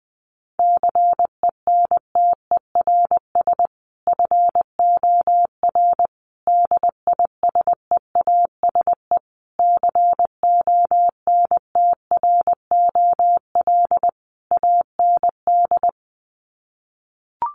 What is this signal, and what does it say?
Morse code, heard as a keyed tone, about 19 wpm, 700 Hz. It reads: CENTERS FOR DISEASE CONTROL AND
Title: Sets of 5 Words Encore2 20wpm